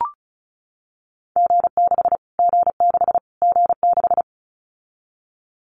Morse code, heard as a keyed tone, about 34 wpm, 700 Hz. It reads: G6 G6 G6
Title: Callsign Prefixes for Top 25 DXCC - Repeated 3x - 35wpm